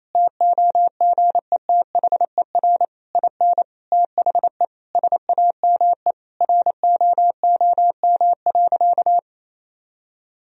Morse code, heard as a keyed tone, about 28 wpm, 700 Hz. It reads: TOGETHER IN THE SAME ROOM.
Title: Sentences from Top 1000 Words - Review Part 3 - 28wpm